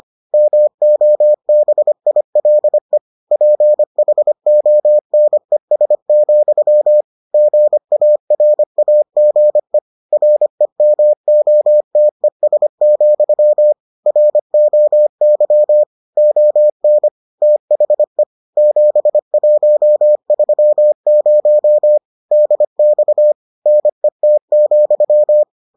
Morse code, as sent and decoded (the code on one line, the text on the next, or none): -- --- -... .. .-.. / .--. .... --- -. ... --..-- / --. .- .-. .- --. / .-. -- --- - . ... --..-- / .-. --- -.-- / --- -. / - .... / --... .---- ...-- ----- / -.. -..- / -. - --..--
MOBILE PHONES, GARAGE REMOTES, ROY ON THE 7130 DX NET,